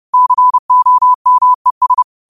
Morse code, heard as a keyed tone, about 30 wpm, 1000 Hz. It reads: COMES
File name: qotdfiles / 30wpm